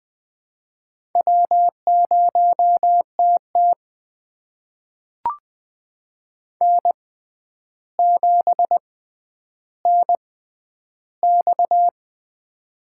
Morse code, heard as a keyed tone, about 21 wpm, spaced as for 10 wpm, 700 Hz. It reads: W0TT N7NX
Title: USA Call Signs - 4 Characters - 6x Character Spacing - 20wpm